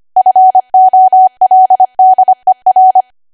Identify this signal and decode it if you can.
Morse code, heard as a keyed tone, about 25 wpm, 750 Hz. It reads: FOLDER